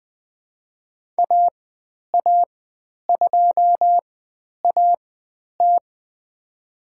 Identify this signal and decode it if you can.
Morse code, heard as a keyed tone, about 20 wpm, spaced as for 10 wpm, 700 Hz. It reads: AA2AT
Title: 095 Lesson 16g - Letter c - Call Signs 20wpm